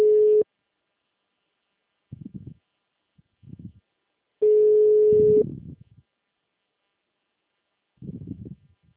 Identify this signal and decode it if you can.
Ringback tone (Europe, 425 Hz).